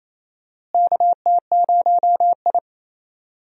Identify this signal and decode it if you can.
Morse code, heard as a keyed tone, about 28 wpm, 700 Hz. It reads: KT0I
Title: USA Call Signs - 4 Characters - Repeated 2x - 28wpm